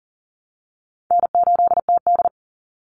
Morse code, heard as a keyed tone, about 40 wpm, 700 Hz. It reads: N8TD
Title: Call Signs - 40wpm